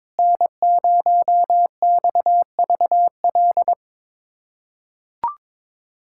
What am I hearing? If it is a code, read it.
Morse code, heard as a keyed tone, about 21 wpm, 700 Hz. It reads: N0XVL
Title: USA Callsign DE Callsign 22wpm